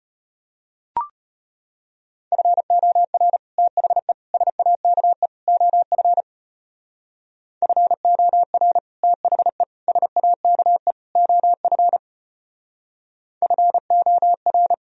Morse code, heard as a keyed tone, about 32 wpm, 700 Hz. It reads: FOR THE SAKE OF FOR THE SAKE OF FOR